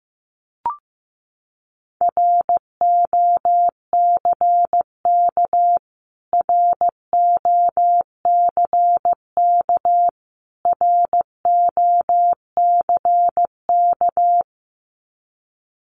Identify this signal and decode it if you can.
Morse code, heard as a keyed tone, about 15 wpm, 700 Hz. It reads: ROCK ROCK ROCK
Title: Top 500 Words Repeated 15wpm